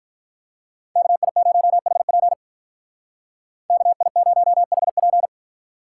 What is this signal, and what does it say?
Morse code, heard as a keyed tone, about 43 wpm, 700 Hz. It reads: KI0HP KI0HP
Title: Speed Racing - USA 2x2 Call Signs 35wpm